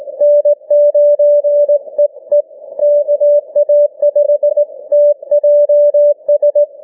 Morse code, heard as a keyed tone, about 18 wpm, 590 Hz. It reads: N9EE KA5TJS
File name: cw / ssn 40M